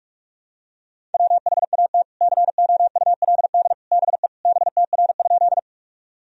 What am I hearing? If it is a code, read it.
Morse code, heard as a keyed tone, about 46 wpm, 700 Hz. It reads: WHAT COULD BE BTR?
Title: Sentences from Top 400 Words and Abbreviations 45wpm